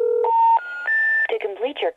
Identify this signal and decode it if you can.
Special information tone, with speech.
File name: fios fe dial area code